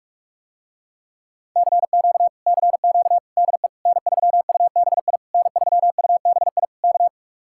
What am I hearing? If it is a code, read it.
Morse code, heard as a keyed tone, about 49 wpm, 700 Hz. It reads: CQ CQ DE N3UBI N3UBI K